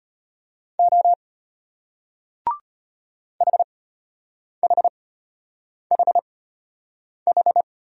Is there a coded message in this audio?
Yes: Morse code from 0.8 to 7.6 s, heard as a keyed tone, about 34 wpm, spaced as for 8 wpm, 700 Hz. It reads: O HHHH